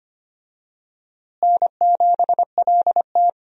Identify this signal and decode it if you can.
Morse code, heard as a keyed tone, about 25 wpm, 700 Hz. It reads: N7LT
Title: USA Call Signs - 4 Characters - Repeated 2x - 25wpm